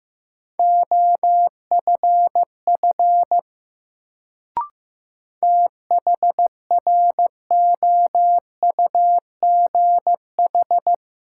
Morse code, heard as a keyed tone, about 15 wpm, 700 Hz. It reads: OFF THROUGH
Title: Sentences from Top 200 Words - New Word Drill - 15wpm